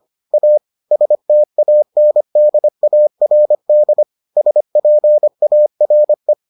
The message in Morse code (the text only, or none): A STANDARD SPARE